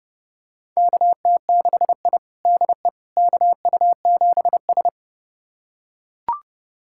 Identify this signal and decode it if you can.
Morse code, heard as a keyed tone, about 31 wpm, 700 Hz. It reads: KT6I DE KU7S
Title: USA Callsign DE Callsign - 4 Characters - 30wpm